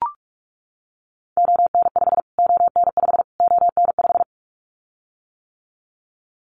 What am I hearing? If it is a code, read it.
Morse code, heard as a keyed tone, about 44 wpm, 700 Hz. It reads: ON5 ON5 ON5
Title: Callsign Prefixes for Top 25 DXCC - Repeated 3x - 45wpm